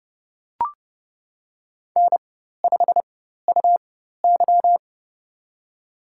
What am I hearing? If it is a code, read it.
Morse code, heard as a keyed tone, about 31 wpm, spaced as for 14 wpm, 700 Hz. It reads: N5UY
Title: USA Call Signs - 4 Characters - 4x Character Spacing - 30wpm